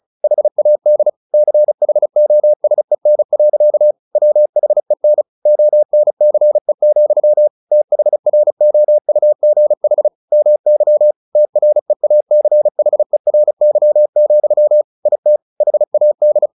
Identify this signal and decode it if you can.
Morse code, heard as a keyed tone, about 35 wpm, 600 Hz. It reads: HAD CHOSEN. WHEN ONCE, THROUGH MY TREACHERY, IT HAD
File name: the old man and the sea / the old man and the sea 1 35wpm